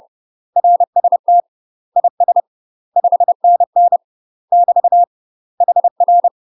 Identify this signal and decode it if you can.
Morse code, heard as a keyed tone, about 30 wpm, 700 Hz. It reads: RST IS 5NN <BT> HR
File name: UK MASTER QSOs TRAINING / 30WPM / UK MASTER QSOs 0015